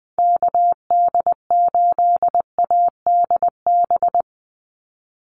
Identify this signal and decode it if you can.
Morse code, heard as a keyed tone, about 20 wpm, 700 Hz. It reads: KD8ADB
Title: USA Call Signs 20wpm